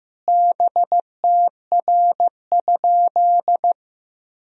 Morse code, heard as a keyed tone, about 15 wpm, 700 Hz. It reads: BTR?
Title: Sentences from Top 400 Words and Abbreviations 15wpm